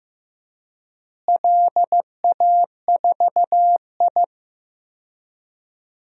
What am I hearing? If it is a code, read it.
Morse code, heard as a keyed tone, about 15 wpm, 700 Hz. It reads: LA4I
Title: DXCC-100 Call Signs - 4 Characters - Repeated 3x - 15wpm